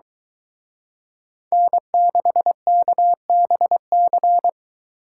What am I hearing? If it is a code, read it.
Morse code, heard as a keyed tone, about 22 wpm, 700 Hz. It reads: N6KBC